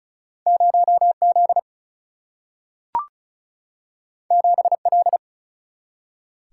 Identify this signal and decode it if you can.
Morse code, heard as a keyed tone, about 36 wpm, 700 Hz. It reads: 0 Z 7 L
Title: Single-Letter Single-Number 35wpm